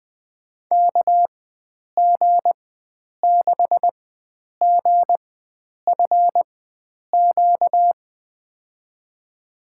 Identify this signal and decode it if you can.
Morse code, heard as a keyed tone, about 21 wpm, spaced as for 9 wpm, 700 Hz. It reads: KG6GFQ
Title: USA Call Signs - 6 Characters - 4x Character Spacing - 20wpm